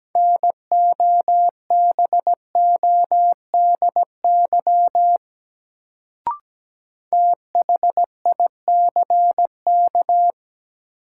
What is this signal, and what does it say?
Morse code, heard as a keyed tone, about 17 wpm, 700 Hz. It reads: NOBODY THICK
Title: Sentences from Top 1600 Words - New Word Drill Alt - 17wpm